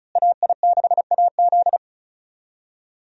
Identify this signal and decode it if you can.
Morse code, heard as a keyed tone, about 35 wpm, 700 Hz. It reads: AI6AZ
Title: USA Call Signs - 5 Characters - Repeated 2x - 35wpm